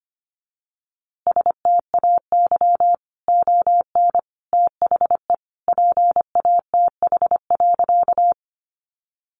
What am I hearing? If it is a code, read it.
Morse code, heard as a keyed tone, about 25 wpm, 700 Hz. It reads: STAY ON THE PATH.
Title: A Sound Of Thunder 25wpm